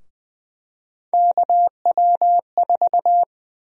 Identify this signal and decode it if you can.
Morse code, heard as a keyed tone, about 20 wpm, 700 Hz. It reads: KW4
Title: USA Call Sign Prefixes - 20wpm